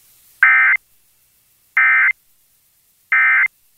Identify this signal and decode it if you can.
Emergency Alert System (SAME) tones, heard as harsh buzzing bursts. It is heard from 0.4 to 3.5 s.